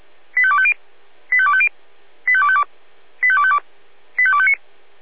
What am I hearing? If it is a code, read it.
Five-tone selective call (ZVEI), heard as quick runs of five tones, tones of about 70 ms. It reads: 85250 85250 85252 85252 85259